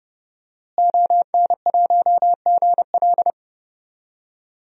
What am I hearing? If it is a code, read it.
Morse code, heard as a keyed tone, about 30 wpm, 700 Hz. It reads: ON1GL
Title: DXCC-100 Call Signs - 5 Characters - Repeated 2x - 30wpm